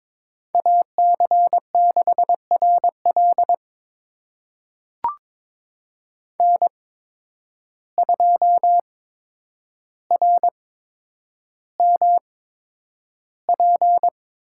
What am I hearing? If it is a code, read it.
Morse code, heard as a keyed tone, about 22 wpm, spaced as for 9 wpm, 700 Hz. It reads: AC6RL N2RMP
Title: USA Call Signs - 5 Characters - 8x Character Spacing - 22wpm